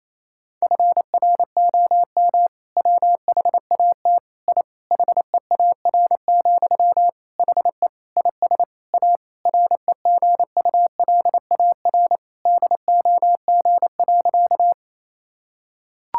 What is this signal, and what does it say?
Morse code, heard as a keyed tone, about 28 wpm, 700 Hz. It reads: FROM WHAT I HEAR, HE IS A REGULAR DOG.
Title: Sentences from Top 1000 Words - Review Part 6 - 28wpm